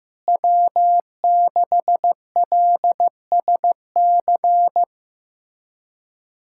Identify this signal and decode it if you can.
Morse code, heard as a keyed tone, about 15 wpm, 700 Hz. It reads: W6LSC
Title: USA Callsign DE Callsign 15wpm